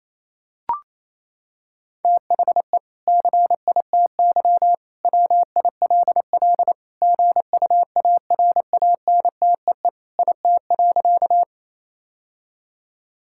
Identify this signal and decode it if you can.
Morse code, heard as a keyed tone, about 28 wpm, 700 Hz. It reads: THE CITY WILL GUARANTEE IT.
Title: Sentences from Top 1600 Words 28wpm